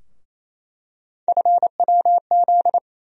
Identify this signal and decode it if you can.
Morse code, heard as a keyed tone, about 28 wpm, 700 Hz. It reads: FWZ
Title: USA Call Sign Postfixes - 3 Letters 28wpm